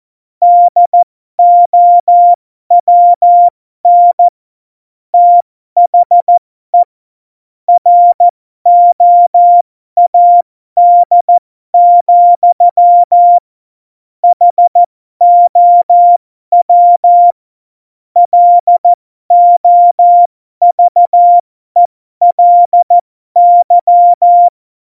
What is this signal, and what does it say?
Morse code, heard as a keyed tone, about 14 wpm, spaced as for 12 wpm, 700 Hz. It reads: DOWN THE ROAD, HOW LOVELY
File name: Morsekurs / Witze